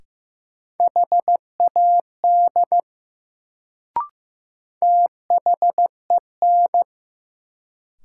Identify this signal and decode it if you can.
Morse code, heard as a keyed tone, about 15 wpm, 700 Hz. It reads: HAD THEN
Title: Sets of 2 Words - New Word Drill - 15wpm